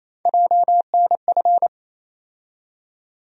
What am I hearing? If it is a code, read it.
Morse code, heard as a keyed tone, about 28 wpm, 700 Hz. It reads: JNF
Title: USA Call Sign Postfixes - 3 Letters 28wpm